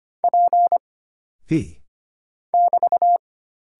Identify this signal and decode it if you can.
Morse code, heard as a keyed tone, about 25 wpm, 700 Hz. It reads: P <BT>
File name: Single Character (Letter:Number:Punctuation:Prosign) - Mind-Melt - 25wpm